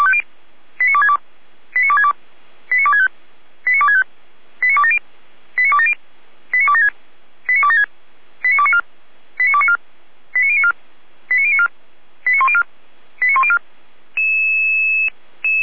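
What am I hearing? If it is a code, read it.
Five-tone selective call (ZVEI), heard as quick runs of five tones, tones of about 70 ms. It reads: 89262 89262 89265 89265 89270 89270 89276 89276 89294 89294 89094 89094 89104 89104
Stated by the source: ZVEI five-tone selective call received on an SDR.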